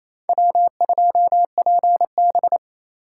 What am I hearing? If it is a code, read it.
Morse code, heard as a keyed tone, about 36 wpm, 700 Hz. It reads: W2PB